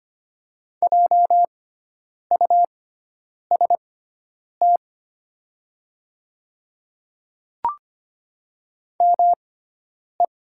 Morse code, heard as a keyed tone, about 25 wpm, spaced as for 9 wpm, 700 Hz. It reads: JUST ME
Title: 6x Character Spacing - Top 200 Words - 25wpm